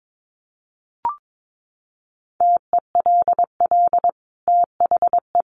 Morse code, heard as a keyed tone, about 22 wpm, 700 Hz. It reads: TELL THE
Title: Sentences from Top 800 Words 22wpm